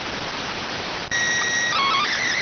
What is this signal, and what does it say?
SSTV (slow-scan television), PD 290.